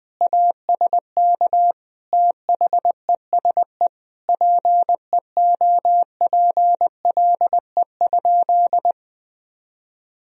Morse code, heard as a keyed tone, about 20 wpm, 700 Hz. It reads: ASK THESE PEOPLE?
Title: Sentences from Top 200 Words and Abbreviations 20wpm